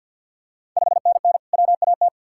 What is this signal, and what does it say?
Morse code, heard as a keyed tone, about 52 wpm, 700 Hz. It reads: 5NN WAT
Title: Ontario QSO Party Exchange 50wpm